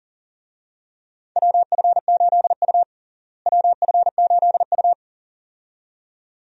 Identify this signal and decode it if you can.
Morse code, heard as a keyed tone, about 40 wpm, 700 Hz. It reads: WF8U WF8U
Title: USA Call Signs - 4 Characters - Repeated 2x - 40wpm